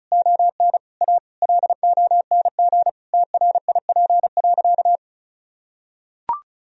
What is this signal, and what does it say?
Morse code, heard as a keyed tone, about 36 wpm, 700 Hz. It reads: ON A LONG TRIP.
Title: Sentences from Top 1000 Words - Review Part 4 - 35wpm